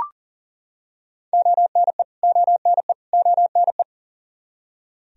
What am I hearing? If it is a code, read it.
Morse code, heard as a keyed tone, about 40 wpm, 700 Hz. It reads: ONE ONE ONE
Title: Top 100 Words Repeated 40wpm